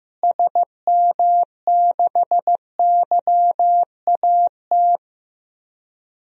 Morse code, heard as a keyed tone, about 15 wpm, 700 Hz. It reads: SM6YAT
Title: DXCC-100 Call Signs - 6 Characters - Repeated 2x - 15wpm